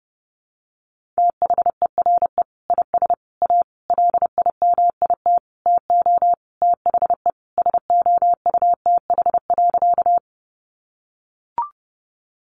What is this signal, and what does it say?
Morse code, heard as a keyed tone, about 30 wpm, 700 Hz. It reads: THERE IS A LIMIT TO THE SOUTH.
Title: Sentences from Top 1500 Words 30wpm